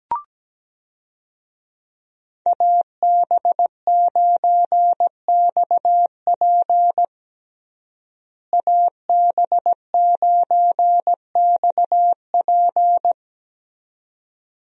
Morse code, heard as a keyed tone, about 17 wpm, 700 Hz. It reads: AB9XP AB9XP
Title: USA Call Signs - 5 Characters - Repeated 2x - 17wpm